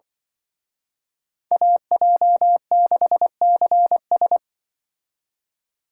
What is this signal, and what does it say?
Morse code, heard as a keyed tone, about 24 wpm, 700 Hz. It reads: AJ6CS
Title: Speed Racing - USA 2x2 Call Signs 20wpm